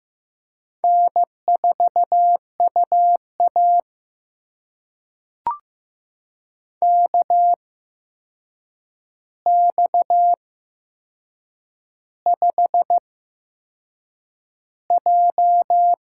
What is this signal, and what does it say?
Morse code, heard as a keyed tone, about 15 wpm, spaced as for 6 wpm, 700 Hz. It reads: N4UA KX5J
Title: USA Call Signs - 4 Characters - 8x Character Spacing - 15wpm